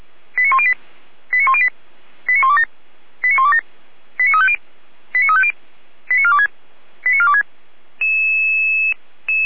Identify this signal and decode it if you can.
Five-tone selective call (ZVEI), heard as quick runs of five tones, tones of about 70 ms. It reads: 89108 89108 89127 89127 89350 89350 89426 89426
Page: SDR/ZVEI